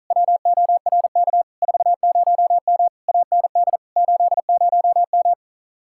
Morse code, heard as a keyed tone, about 44 wpm, 700 Hz. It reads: WORK 40M AND 80M
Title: Speed-Racing Common QSO Phrases Encore - 35wpm